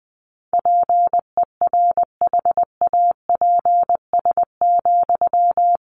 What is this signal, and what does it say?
Morse code, heard as a keyed tone, about 20 wpm, 700 Hz. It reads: PERHAPS,
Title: Sentences from Top 500 Words 20wpm